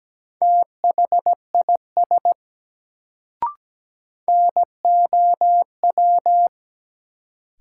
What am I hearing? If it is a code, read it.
Morse code, heard as a keyed tone, about 17 wpm, 700 Hz. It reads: THIS NOW
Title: Sentences from Top 100 Words - New Word Drill - 17wpm